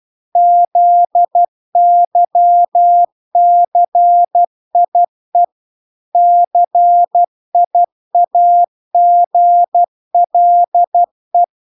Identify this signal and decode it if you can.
Morse code, heard as a keyed tone, about 12 wpm, 700 Hz. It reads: ZYCIE CIAGLE